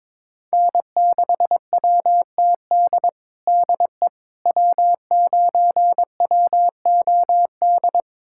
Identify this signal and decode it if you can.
Morse code, heard as a keyed tone, about 21 wpm, 700 Hz. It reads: N6WTD DE W9WOD
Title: USA Callsign DE Callsign - 5 Characters - 22wpm